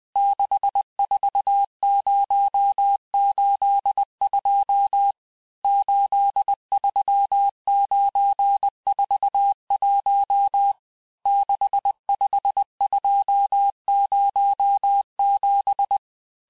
Morse code, heard as a keyed tone, about 20 wpm, 800 Hz. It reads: 64082 83941 65207
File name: cours-1584-teste-chiffre-vitesse-19mpm